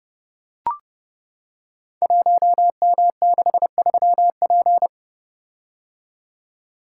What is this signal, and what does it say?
Morse code, heard as a keyed tone, about 30 wpm, 700 Hz. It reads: 1M63P
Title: Code Groups - 5 Letter:Numbers - 30wpm